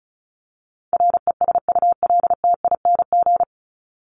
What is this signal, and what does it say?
Morse code, heard as a keyed tone, about 34 wpm, 700 Hz. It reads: RESULTING
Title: ING Suffix - Top 100 Words with Base Words 35wpm